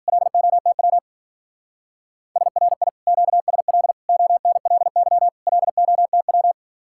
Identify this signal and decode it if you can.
Morse code, heard as a keyed tone, about 52 wpm, 700 Hz. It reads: LOTW SRI QSL ONLY LOTW